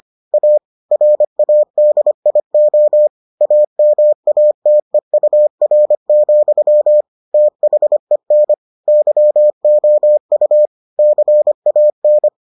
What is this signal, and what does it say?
Morse code, heard as a keyed tone, about 25 wpm, 600 Hz. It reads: A RADIO AMATEUR, THEN YOU CAN